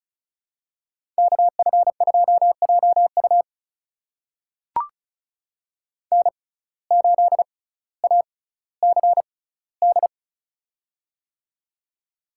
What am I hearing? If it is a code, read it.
Morse code, heard as a keyed tone, about 36 wpm, spaced as for 18 wpm, 700 Hz. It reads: KF2JU N8ACD
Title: USA Call Signs - 5 Characters - 6x Character Spacing - 35wpm